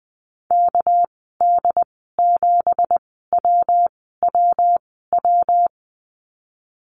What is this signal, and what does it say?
Morse code, heard as a keyed tone, about 20 wpm, spaced as for 14 wpm, 700 Hz. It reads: KD7WWW
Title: 245 Lesson 39g - Letter z - Call Signs 20wpm